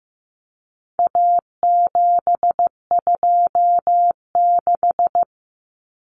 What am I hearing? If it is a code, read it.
Morse code, heard as a keyed tone, about 15 wpm, 700 Hz. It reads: A726
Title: Common Cut Numbers - 4-Digit 15wpm